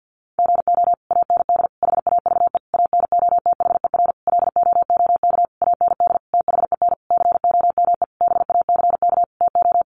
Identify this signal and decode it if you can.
Morse code, heard as a keyed tone, about 50 wpm, 700 Hz. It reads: GO AND HAVE ANOTHER LOOK AND THEN COME BACK TO
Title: Sentences from Top 600 Words 50wpm